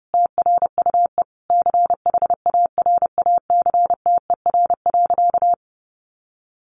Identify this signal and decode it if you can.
Morse code, heard as a keyed tone, about 30 wpm, 700 Hz. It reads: TRUE CHARACTER.
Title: Sentences from Top 1200 Words 30wpm